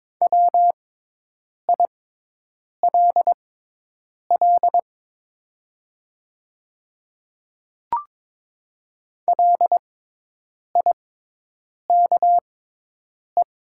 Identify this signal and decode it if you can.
Morse code, heard as a keyed tone, about 22 wpm, spaced as for 8 wpm, 690 Hz. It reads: WILL LIKE
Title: 6x Character Spacing - Top 100 Words - 22wpm